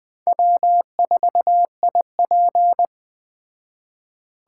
Morse code, heard as a keyed tone, about 20 wpm, 700 Hz. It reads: W4IP